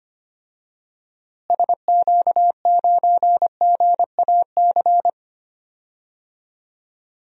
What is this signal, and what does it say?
Morse code, heard as a keyed tone, about 25 wpm, 700 Hz. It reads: SQ9GAC
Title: DXCC-100 Call Signs - 6 Characters - Repeated 3x - 25wpm